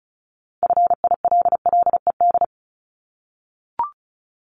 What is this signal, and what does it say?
Morse code, heard as a keyed tone, about 36 wpm, 700 Hz. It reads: FILLED
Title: Multi-letter Phonemes 35wpm